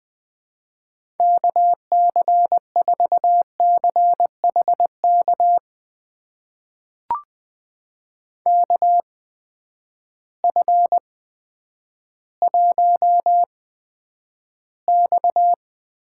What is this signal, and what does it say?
Morse code, heard as a keyed tone, about 20 wpm, spaced as for 10 wpm, 700 Hz. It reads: KC4CHK KF1X
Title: USA Call Signs - 8x Character Spacing - 20wpm